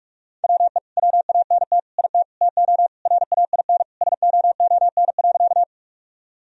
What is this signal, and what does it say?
Morse code, heard as a keyed tone, about 46 wpm, 700 Hz. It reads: WE WANT IT TO RAIN SOON.
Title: Sentences from Top 1000 Words - Review Part 1 - 45wpm